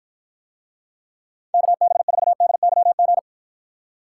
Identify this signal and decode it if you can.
Morse code, heard as a keyed tone, about 53 wpm, 700 Hz. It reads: KB4DYG